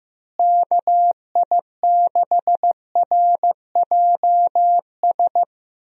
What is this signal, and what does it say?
Morse code, heard as a keyed tone, about 15 wpm, 700 Hz. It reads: KI6RJS
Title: USA Call Signs - 6 Characters - Repeated 2x - 15wpm